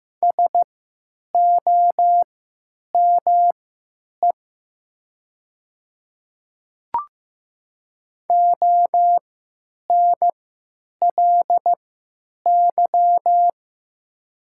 Morse code, heard as a keyed tone, about 15 wpm, spaced as for 9 wpm, 700 Hz. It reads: SOME ONLY
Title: 3x Character Spacing - Top 500 Words - 15wpm